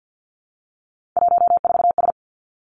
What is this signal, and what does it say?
Morse code, heard as a keyed tone, about 50 wpm, 700 Hz. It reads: 14S
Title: Three Challenging Letter-Number Combinations 50wpm